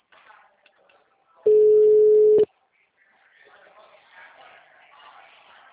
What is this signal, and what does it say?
Ringback tone (Europe, 425 Hz).